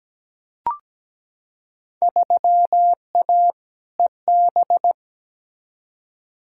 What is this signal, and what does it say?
Morse code, heard as a keyed tone, about 17 wpm, 700 Hz. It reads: 3A EB
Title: Field Day Exchanges Encore - 17wpm